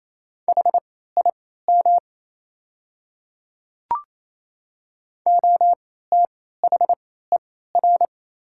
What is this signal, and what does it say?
Morse code, heard as a keyed tone, about 28 wpm, spaced as for 16 wpm, 700 Hz. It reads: HIM OTHER
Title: 3x Character Spacing - Top 200 Words - 28wpm